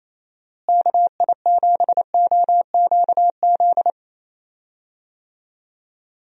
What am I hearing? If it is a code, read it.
Morse code, heard as a keyed tone, about 28 wpm, 700 Hz. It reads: KI7OQZ
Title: USA Call Signs - 6 Characters - Repeated 3x - 28wpm